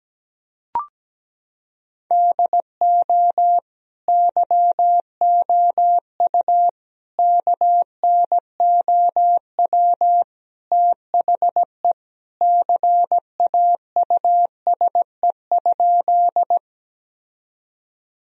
Morse code, heard as a keyed tone, about 17 wpm, 700 Hz. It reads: DO YOU KNOW THE CAUSE?
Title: Sentences from Top 1000 Words - Review Part 1 - 17wpm